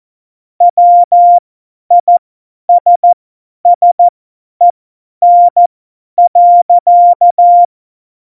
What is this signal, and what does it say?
Morse code, heard as a keyed tone, about 14 wpm, spaced as for 10 wpm, 700 Hz. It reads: WISSEN.
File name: Morsekurs / Witze / Witz 21. Mai 2019, 50 F70